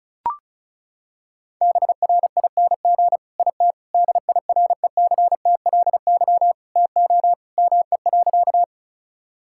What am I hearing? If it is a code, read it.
Morse code, heard as a keyed tone, about 34 wpm, 700 Hz. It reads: BRING IT DIRECTLY TO ME.
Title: Sentences from Top 1000 Words - Review Part 5 - 35wpm